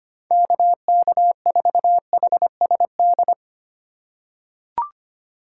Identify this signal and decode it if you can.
Morse code, heard as a keyed tone, about 25 wpm, 700 Hz. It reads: KK4HSD
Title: USA Call Signs - 4x Character Spacing - 25wpm